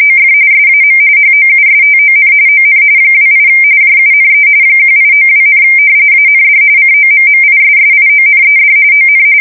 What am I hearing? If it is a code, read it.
RTTY, about 45.45 baud. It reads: OF SAM HOUSTON ON 28 JUNE 1892, AND COMMISSIONED ON
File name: fsk170 / battleships